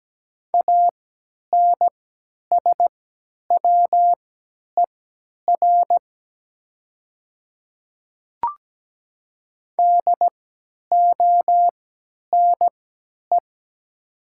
Morse code, heard as a keyed tone, about 17 wpm, spaced as for 10 wpm, 700 Hz. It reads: ANSWER DONE
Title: 3x Character Spacing - Top 500 Words - 17wpm